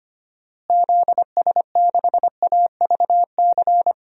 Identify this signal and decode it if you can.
Morse code, heard as a keyed tone, about 25 wpm, 700 Hz. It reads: ZS6AVC
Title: DXCC-100 Call Signs - 6 Characters - Repeated 3x - 25wpm